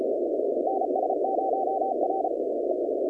Morse code, heard as a keyed tone, about 34 wpm, 760 Hz, with noise. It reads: NS0R
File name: contests / 2019-cw-wpx-kt5j-k5pi- -2330 0-20